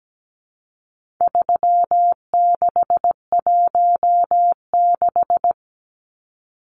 Morse code, heard as a keyed tone, about 17 wpm, 700 Hz. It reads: 3616
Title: Numbers - 4-Digit 17wpm